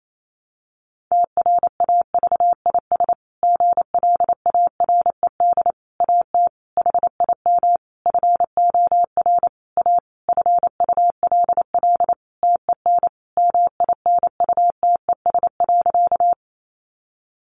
Morse code, heard as a keyed tone, about 28 wpm, 700 Hz. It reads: TRAVIS GLARED AT HIM FOR A FULL TEN MINUTES.
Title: A Sound Of Thunder 28wpm